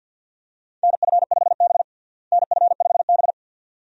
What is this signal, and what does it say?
Morse code, heard as a keyed tone, about 51 wpm, 700 Hz. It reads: NF5B NF5B
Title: USA Call Signs - 4 Characters - Repeated 2x - 50wpm